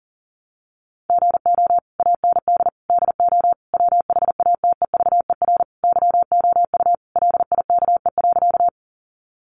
Morse code, heard as a keyed tone, about 37 wpm, 700 Hz. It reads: GO AND DO WHATEVER YOU LIKE.
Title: Sentences from Top 1100 Words 40wpm